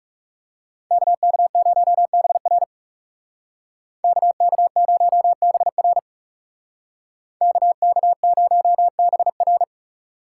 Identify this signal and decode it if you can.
Morse code, heard as a keyed tone, about 37 wpm, 700 Hz. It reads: KK0BR KK0BR KK0BR
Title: Speed Racing - USA 2x2 Call Signs 30wpm